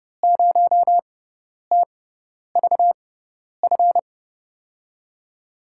Morse code, heard as a keyed tone, about 30 wpm, spaced as for 10 wpm, 700 Hz. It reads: 0TVF
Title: USA Call Signs - 6 Characters - 6x Character Spacing - 30wpm